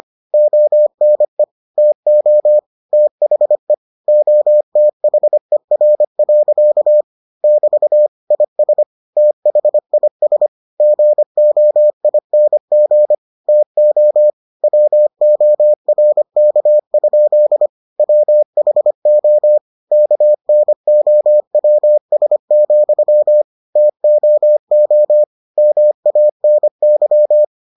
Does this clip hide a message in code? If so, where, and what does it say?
Yes: Morse code from 0.3 to 27.4 s, heard as a keyed tone, about 25 wpm, 600 Hz. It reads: ONE TO THE OTHER. <BT> IS THIS GOING TO WORK? WHO KNOWS, TOO MANY